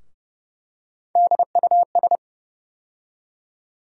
Morse code, heard as a keyed tone, about 30 wpm, 700 Hz. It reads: DUS
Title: USA Call Sign Postfixes - 3 Letters 30wpm